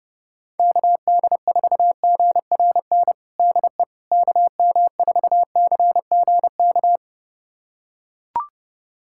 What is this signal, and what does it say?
Morse code, heard as a keyed tone, about 32 wpm, 700 Hz. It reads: KD4GRN DE KM4CGK
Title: USA Callsign DE Callsign - 6 Characters - 30wpm